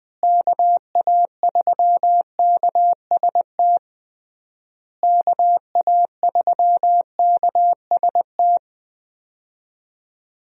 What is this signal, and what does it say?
Morse code, heard as a keyed tone, about 21 wpm, 700 Hz. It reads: KA3KST KA3KST
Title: USA Call Signs - 6 Characters - Repeated 3x - 20wpm